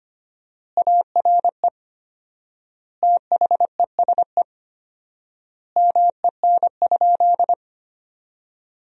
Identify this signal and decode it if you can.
Morse code, heard as a keyed tone, about 25 wpm, 700 Hz. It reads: ARE THESE MEN?
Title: 4x Word-Spacing - Sentences from Top 200 Words - 25wpm